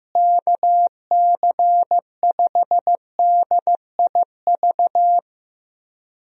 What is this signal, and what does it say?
Morse code, heard as a keyed tone, about 15 wpm, 700 Hz. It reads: KC5DIV
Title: USA Call Signs - 6 Characters - Repeated 2x - 15wpm